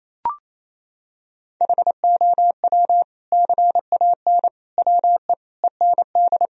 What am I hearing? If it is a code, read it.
Morse code, heard as a keyed tone, about 28 wpm, 700 Hz. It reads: HOW CAN WE END
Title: Sentences from Top 1000 Words - Review Part 2 - 28wpm